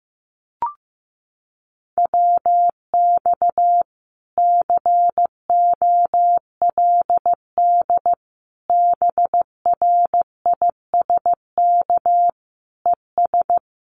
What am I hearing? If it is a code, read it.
Morse code, heard as a keyed tone, about 15 wpm, 700 Hz. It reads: WX COLD BRISK ES
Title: Common QSO Phrases Encore 15wpm